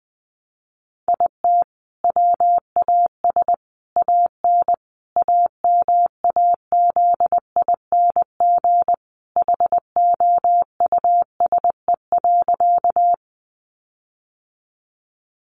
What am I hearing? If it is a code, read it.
Morse code, heard as a keyed tone, about 20 wpm, 700 Hz. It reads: IT WAS AN AMAZING HOUSE.
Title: Sentences from Top 1500 Words 20wpm